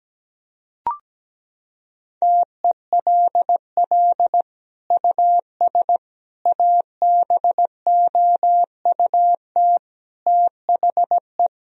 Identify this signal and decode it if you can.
Morse code, heard as a keyed tone, about 17 wpm, 700 Hz. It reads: TELL US ABOUT THE
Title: Sentences from Top 1000 Words - Review Part 4 - 17wpm